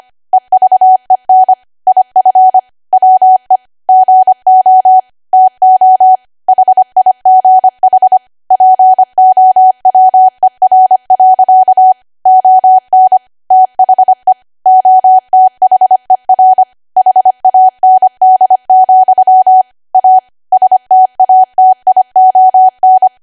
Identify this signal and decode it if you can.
Morse code, heard as a keyed tone, about 25 wpm, 750 Hz. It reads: EVEN IF WE GO TO HIGH POWER. ON THE OTHER HAND, A STATION
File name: Archive / 25 WPM / 25WPM